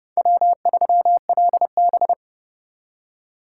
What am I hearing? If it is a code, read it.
Morse code, heard as a keyed tone, about 38 wpm, 700 Hz. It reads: W3LB